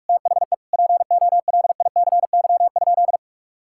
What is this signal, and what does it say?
Morse code, heard as a keyed tone, about 45 wpm, 700 Hz. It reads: THE POLICY?
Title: Sentences from Top 1000 Words - Review Part 5 - 45wpm